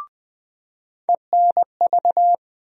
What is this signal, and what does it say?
Morse code, heard as a keyed tone, about 20 wpm, 700 Hz. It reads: ENV
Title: USA Call Sign Postfixes - 3 Letters 20wpm